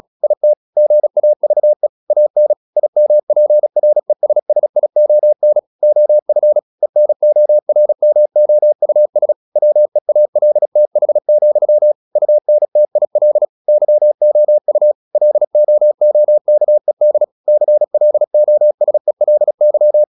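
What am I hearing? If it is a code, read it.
Morse code, heard as a keyed tone, about 36 wpm, 600 Hz. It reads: IT GAVE AN IMPRESSION OF ENORMOUS WEALTH, UNTIL YOU LOOKED CLOSELY
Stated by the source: Morse practice at 36 wpm, text of Ken Follett's novel Never, from CW med Gnister.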